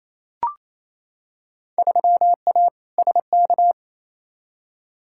Morse code, heard as a keyed tone, about 28 wpm, 700 Hz. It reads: 3A SK
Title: Field Day Exchanges Encore - 28wpm